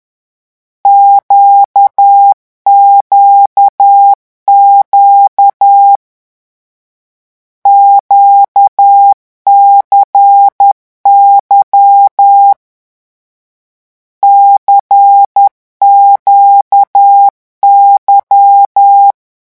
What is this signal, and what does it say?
Morse code, heard as a keyed tone, about 10 wpm, 800 Hz. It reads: QQQ QCY CQY
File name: morse mp3 / lecon2-vitesse 10